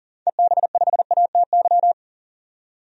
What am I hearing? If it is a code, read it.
Morse code, heard as a keyed tone, about 40 wpm, 700 Hz. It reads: EB5ATY
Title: DXCC-100 Call Signs - 6 Characters - 40wpm